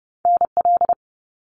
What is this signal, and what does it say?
Morse code, heard as a keyed tone, about 31 wpm, 700 Hz. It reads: NL
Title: Canadian Provinces and Territories 30wpm